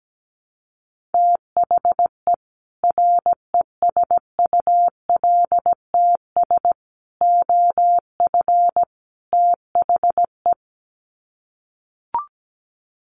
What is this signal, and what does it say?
Morse code, heard as a keyed tone, about 17 wpm, 700 Hz. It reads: THE RESULTS OF THE
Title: Sets of 4 Words 17wpm